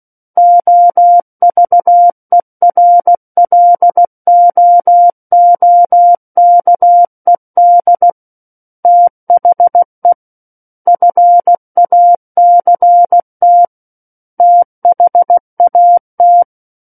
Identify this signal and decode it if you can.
Morse code, heard as a keyed tone, about 16 wpm, 700 Hz. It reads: OVERLOOKED THE FACT THAT
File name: Book One / 07-Chapter 7 - 16 WPM